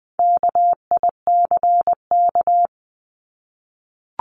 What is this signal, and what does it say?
Morse code, heard as a keyed tone, about 20 wpm, 700 Hz. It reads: KICK
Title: Top 1251-1500 Words 20wpm